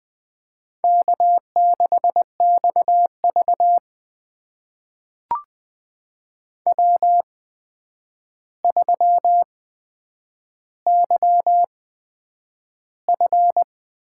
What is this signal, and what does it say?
Morse code, heard as a keyed tone, about 20 wpm, spaced as for 9 wpm, 700 Hz. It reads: K6XV W3YF
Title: USA Call Signs - 4 Characters - 8x Character Spacing - 20wpm